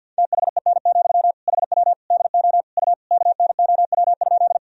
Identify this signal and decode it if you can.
Morse code, heard as a keyed tone, about 47 wpm, 700 Hz. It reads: THEN, HW DO U KNOW?
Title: Sentences from Top 100 Words and Abbreviations 50wpm